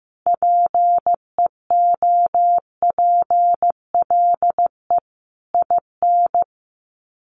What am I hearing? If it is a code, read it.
Morse code, heard as a keyed tone, about 15 wpm, 700 Hz. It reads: PEOPLE IN
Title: Sets of 5 Words 15wpm